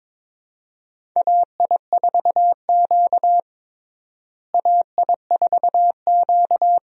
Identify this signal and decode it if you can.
Morse code, heard as a keyed tone, about 21 wpm, 700 Hz. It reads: AI4Q AI4Q
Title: USA Call Signs - 4 Characters - Repeated 2x - 22wpm